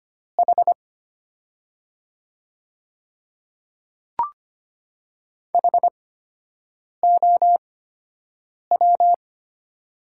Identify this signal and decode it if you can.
Morse code, heard as a keyed tone, about 25 wpm, spaced as for 7 wpm, 700 Hz. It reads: H HOW
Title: 8x Character Spacing - Top 100 Words - 25wpm